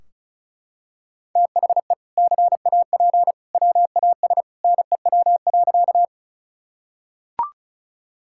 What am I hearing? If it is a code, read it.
Morse code, heard as a keyed tone, about 34 wpm, 700 Hz. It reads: THE CAP WAS NEW.
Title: Sentences from Top 1600 Words 35wpm